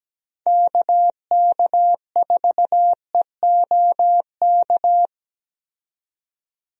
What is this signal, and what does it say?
Morse code, heard as a keyed tone, about 17 wpm, 700 Hz. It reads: KK4EOK
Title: USA Call Signs - 6 Characters - 17wpm